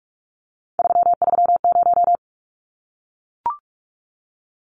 Morse code, heard as a keyed tone, about 44 wpm, 700 Hz. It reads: 330
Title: Numbers - 3-Digit 45wpm